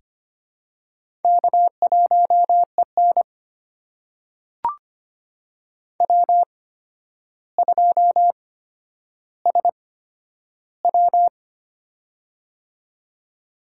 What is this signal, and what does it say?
Morse code, heard as a keyed tone, about 25 wpm, spaced as for 11 wpm, 700 Hz. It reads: K1EN W2SW
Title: USA Call Signs - 4 Characters - 8x Character Spacing - 25wpm